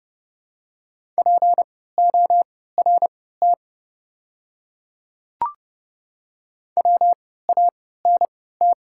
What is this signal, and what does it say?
Morse code, heard as a keyed tone, about 31 wpm, spaced as for 17 wpm, 700 Hz. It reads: PORT WANT
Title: 3x Character Spacing - Top 500 Words - 30wpm